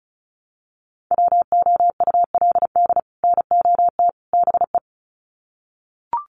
Morse code, heard as a keyed tone, about 34 wpm, 700 Hz. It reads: WOULD NOT BE
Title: Sets of 3 Words 35wpm